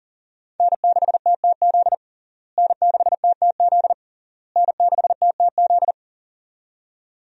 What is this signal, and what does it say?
Morse code, heard as a keyed tone, about 37 wpm, 700 Hz. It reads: N6TTZ N6TTZ N6TTZ
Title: USA Call Signs - 5 Characters - Repeated 3x - 40wpm